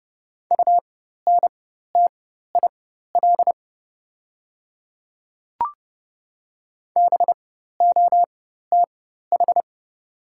Morse code, heard as a keyed tone, about 30 wpm, spaced as for 14 wpm, 690 Hz. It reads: UNTIL BOTH
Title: 4x Character Spacing - Top 300 Words - 30wpm